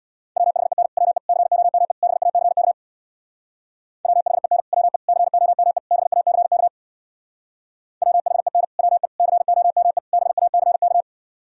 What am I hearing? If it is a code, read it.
Morse code, heard as a keyed tone, about 64 wpm, 700 Hz. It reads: WHEN WE COME BACK WHEN WE COME BACK WHEN WE COME BACK